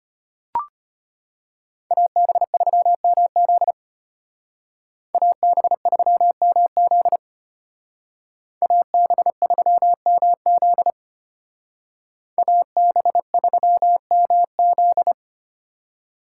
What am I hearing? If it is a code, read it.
Morse code, heard as a keyed tone, about 30 wpm, 700 Hz. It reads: AB3MZ AB3MZ AB3MZ AB3MZ
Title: Speed Racing - USA Call Signs 25wpm